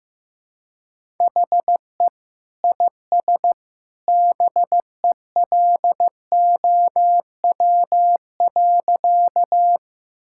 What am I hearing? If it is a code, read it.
Morse code, heard as a keyed tone, about 16 wpm, 700 Hz. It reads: HE IS BELOW.
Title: Sentences from Top 1000 Words - Review Part 5 - 15wpm